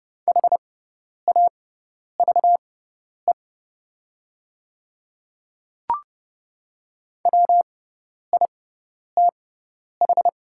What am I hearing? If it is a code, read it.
Morse code, heard as a keyed tone, about 31 wpm, spaced as for 10 wpm, 700 Hz. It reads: HAVE WITH
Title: 6x Character Spacing - Top 100 Words - 30wpm